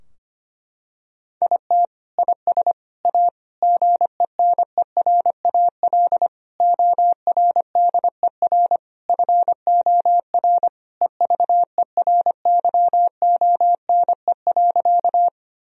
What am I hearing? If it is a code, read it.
Morse code, heard as a keyed tone, about 25 wpm, 700 Hz. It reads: IT IS A GENERAL ORDER FOR EVERYONE.
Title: Sentences from Top 1000 Words - Review Part 3 - 25wpm